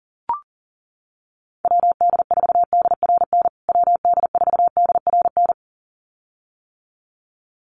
Morse code, heard as a keyed tone, about 40 wpm, 700 Hz. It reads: WD4DRN WD4DRN
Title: USA Call Signs Repeated 2x 40wpm